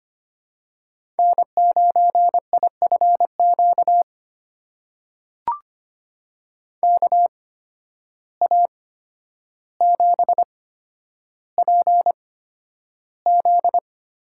Morse code, heard as a keyed tone, about 25 wpm, spaced as for 11 wpm, 700 Hz. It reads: N9IFQ KA7PZ
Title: USA Call Signs - 8x Character Spacing - 25wpm